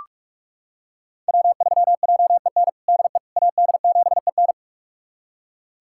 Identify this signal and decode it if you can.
Morse code, heard as a keyed tone, about 46 wpm, 700 Hz. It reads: W3JEN DE AD7EN
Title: USA Callsign DE Callsign - 5 Characters - 45wpm